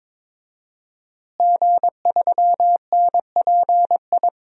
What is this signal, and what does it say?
Morse code, heard as a keyed tone, about 22 wpm, 700 Hz. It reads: G3NPI
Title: DXCC-100 Call Signs - 5 Characters - Repeated 2x - 22wpm